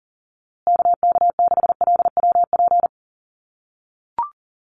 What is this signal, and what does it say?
Morse code, heard as a keyed tone, about 40 wpm, 700 Hz. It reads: KK6LWP
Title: USA Call Signs Repeated 2x 40wpm